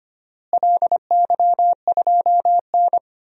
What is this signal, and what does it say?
Morse code, heard as a keyed tone, about 25 wpm, 700 Hz. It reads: LY2N
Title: DXCC-100 Call Signs - 4 Characters - Repeated 2x - 25wpm